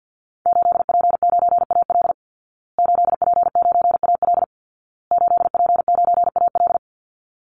Morse code, heard as a keyed tone, about 50 wpm, 700 Hz. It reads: 8P9AL 8P9AL 8P9AL
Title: Call Signs Repeated 2x - 50wpm